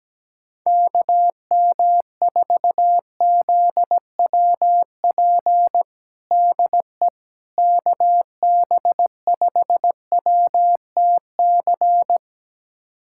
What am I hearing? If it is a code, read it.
Morse code, heard as a keyed tone, about 17 wpm, 700 Hz. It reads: KM4ZWP DE KB5WTC
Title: USA Callsign DE Callsign 17wpm